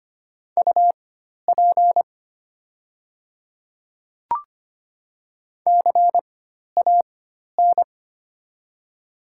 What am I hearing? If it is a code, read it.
Morse code, heard as a keyed tone, about 25 wpm, spaced as for 12 wpm, 700 Hz. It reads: UP CAN
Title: 4x Character Spacing - Top 100 Words - 25wpm